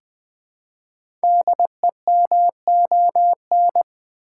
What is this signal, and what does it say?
Morse code, heard as a keyed tone, about 20 wpm, 700 Hz. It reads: DEMON
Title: Build A Word - 1 Letter At A Time - 5-Letter Words - 20wpm